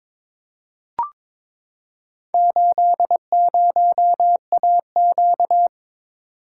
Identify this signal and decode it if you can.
Morse code, heard as a keyed tone, about 22 wpm, 700 Hz. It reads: 80AQ
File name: practice / 22wpm / Four Letter-Number Combinations - 22wpm